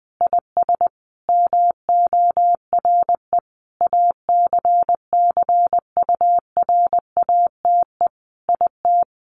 Morse code, heard as a keyed tone, about 20 wpm, 700 Hz. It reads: IS MORE ACCURATE IT
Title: Sentences from Top 1200 Words 20wpm